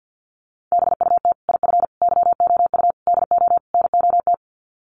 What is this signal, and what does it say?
Morse code, heard as a keyed tone, about 50 wpm, 700 Hz. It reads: BUT IF YOU DO NOT
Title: Sets of 5 Words Encore 50wpm